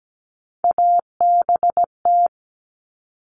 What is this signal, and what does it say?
Morse code, heard as a keyed tone, about 17 wpm, 700 Hz. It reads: ABT
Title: Abbreviations 17wpm